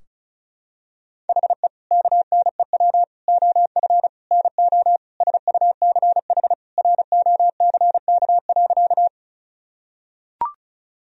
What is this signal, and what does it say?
Morse code, heard as a keyed tone, about 36 wpm, 700 Hz. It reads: HE KNEW OF NO SUCH ROCK.
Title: Sentences from Top 1000 Words - Review Part 2 - 35wpm